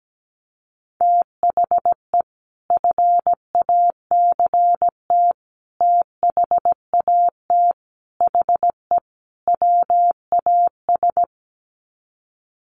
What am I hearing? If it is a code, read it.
Morse code, heard as a keyed tone, about 17 wpm, 700 Hz. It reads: THE FACT THAT HE WAS
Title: Sets of 5 Words 17wpm